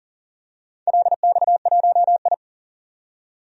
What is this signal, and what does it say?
Morse code, heard as a keyed tone, about 40 wpm, 700 Hz. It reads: LX1I
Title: DXCC-100 Call Signs - 4 Characters - 40wpm